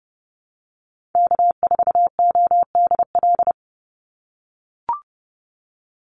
Morse code, heard as a keyed tone, about 30 wpm, 700 Hz. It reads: K4ODL
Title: Call Signs - 30wpm